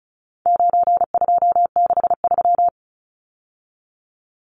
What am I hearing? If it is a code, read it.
Morse code, heard as a keyed tone, about 34 wpm, 700 Hz. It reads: 9263
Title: Numbers - 4-Digit 35wpm